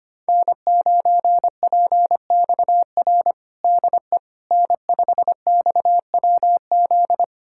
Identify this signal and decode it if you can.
Morse code, heard as a keyed tone, about 25 wpm, 700 Hz. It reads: N9PXR DE N5XWZ
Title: USA Callsign DE Callsign - 5 Characters - 25wpm